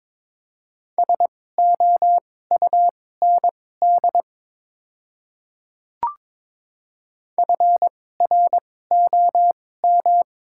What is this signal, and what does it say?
Morse code, heard as a keyed tone, about 22 wpm, spaced as for 16 wpm, 700 Hz. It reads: SOUND FROM
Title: 2x Character Spacing - Top 100 Words - 22wpm